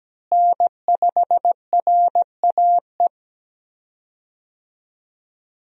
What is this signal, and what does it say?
Morse code, heard as a keyed tone, about 17 wpm, 700 Hz. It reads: N5RAE
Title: USA Call Signs - 5 Characters - Repeated 3x - 17wpm